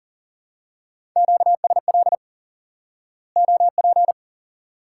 Morse code, heard as a keyed tone, about 40 wpm, 700 Hz. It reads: QSL OP
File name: IQE Course 3 - Lesson 89 - YAGI - Two Elements - 40wpm